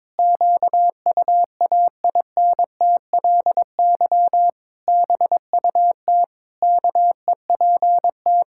Morse code, heard as a keyed tone, about 22 wpm, 700 Hz. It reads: QUAINTLY BUT KEPT
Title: Pangram Challenge v2 - 22wpm